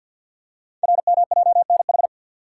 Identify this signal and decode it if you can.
Morse code, heard as a keyed tone, about 48 wpm, 690 Hz. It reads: RMJNH
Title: Code Groups - 5 Letters - 50wpm